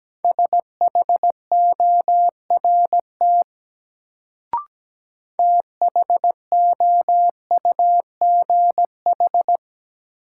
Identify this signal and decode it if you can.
Morse code, heard as a keyed tone, about 17 wpm, 700 Hz. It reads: SHORT THOUGH
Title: Sentences from Top 400 Words - New Word Drill - 17wpm